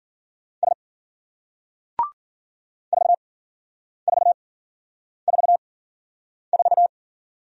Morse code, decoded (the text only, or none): S 4444